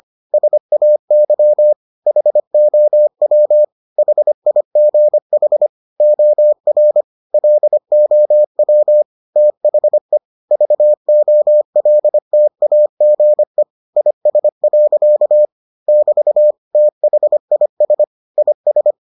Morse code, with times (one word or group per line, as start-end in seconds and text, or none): 0.3-1.7 SAY
2.1-3.6 HOW
4.0-5.7 HIGH
6.0-7.0 OR
7.3-9.0 LOW
9.4-10.2 THE
10.5-13.6 VOLTAGE
14.0-15.4 IS.
15.9-16.5 <BT>
16.8-18.0 THIS
18.4-18.9 IS